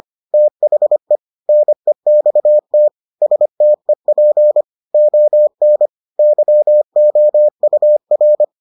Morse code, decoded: THE NEXT STEP ON YOUR

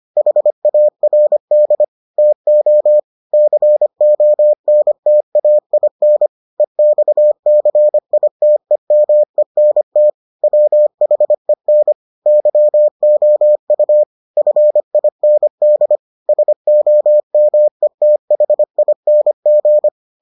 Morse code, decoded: HARD TO CONTAIN EXCITEMENT WHEN YOU FIND SOMETHING

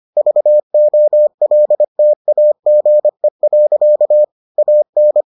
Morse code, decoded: VOLTAGE. AN

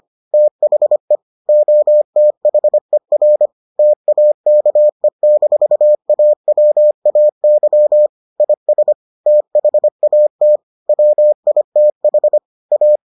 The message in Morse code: THE OTHER TAKE-AWAY IS THAT WITH A

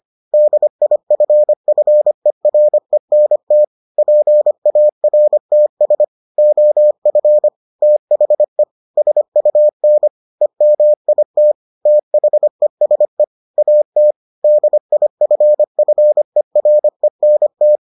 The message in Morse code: DIFFERENT PARTS OF THE SUN EMIT THESE AT DIFFERENT

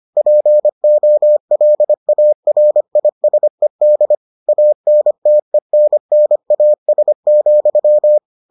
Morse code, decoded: POLARISED ANTENNAS,

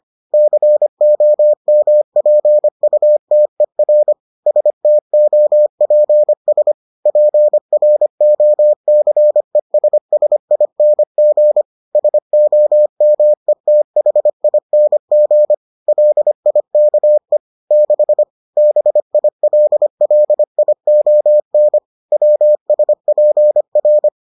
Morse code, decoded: COMPUTER STOPS PROCESSING SOMETHING LIKE 6 BILLION WSPR

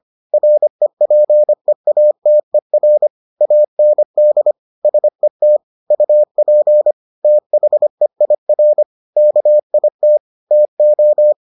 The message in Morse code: REPEATER AND SET UP THEIR KIT TO

